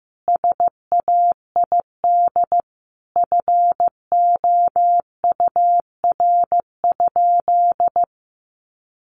Morse code, heard as a keyed tone, about 15 wpm, 700 Hz. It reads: SAID FOUR?
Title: Sentences from Top 300 Words 15wpm